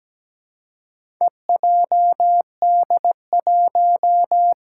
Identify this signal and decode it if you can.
Morse code, heard as a keyed tone, about 17 wpm, 700 Hz. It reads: EJD1
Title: Four Challenging Letter-Number Combinations 17wpm